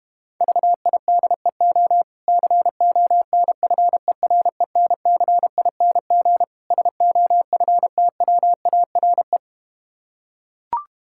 Morse code, heard as a keyed tone, about 32 wpm, 700 Hz. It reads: VIDEO CONFERENCING SOFTWARE